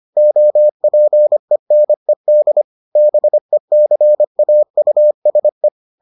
Morse code, heard as a keyed tone, about 25 wpm, 600 Hz. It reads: OPENED BECAUSE